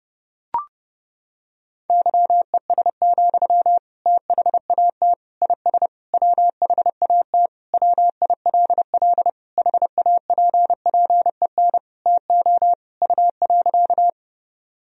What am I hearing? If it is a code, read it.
Morse code, heard as a keyed tone, about 32 wpm, 700 Hz. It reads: YES, THAT IS WHAT WILL HAPPEN TO U.
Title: Sentences from Top 500 Words and Abbreviations 30wpm